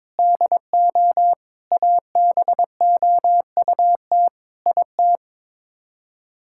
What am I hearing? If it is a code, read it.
Morse code, heard as a keyed tone, about 22 wpm, 700 Hz. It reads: DO ABOUT IT
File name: practice / 22wpm / Sets of 8 Words - 22wpm